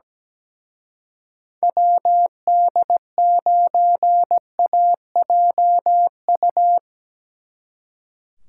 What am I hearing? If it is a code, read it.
Morse code, heard as a keyed tone, about 19 wpm, 700 Hz. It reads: WD9AJU
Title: Speed Racing - USA 2x3 Call Signs 17wpm